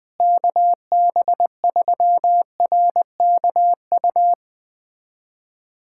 Morse code, heard as a keyed tone, about 20 wpm, 700 Hz. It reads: KB3RKU
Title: USA Callsign DE Callsign 20wpm